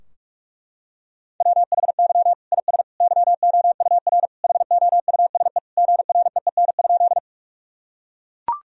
Morse code, heard as a keyed tone, about 41 wpm, 700 Hz. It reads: WHY IS YOUR HOUSE GREEN?
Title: Sentences from Top 1000 Words - Review Part 2 - 45wpm